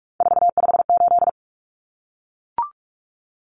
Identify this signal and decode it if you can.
Morse code, heard as a keyed tone, about 44 wpm, 700 Hz. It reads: 458
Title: Numbers - 3-Digit 45wpm